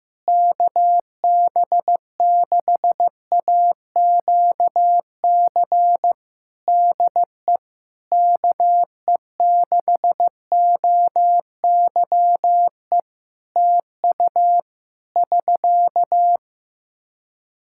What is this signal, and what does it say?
Morse code, heard as a keyed tone, about 16 wpm, 700 Hz. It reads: KB6AQC DE KE6OYE TU <SK>
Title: Practice QSO - Exchange 4 - Ending - 15wpm